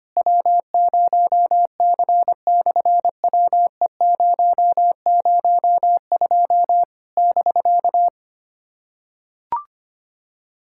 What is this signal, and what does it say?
Morse code, heard as a keyed tone, about 25 wpm, 700 Hz. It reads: W0C/WE002 <BK>
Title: SOTA QSO Practice 25wpm